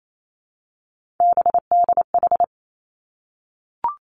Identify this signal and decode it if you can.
Morse code, heard as a keyed tone, about 28 wpm, 700 Hz. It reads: BDH
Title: Three Challenging Letter-Number Combinations 28wpm